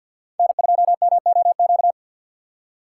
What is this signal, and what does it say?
Morse code, heard as a keyed tone, about 49 wpm, 700 Hz. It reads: N2MOQ
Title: USA Call Signs - 5 Characters - 2x Character Spacing - 50wpm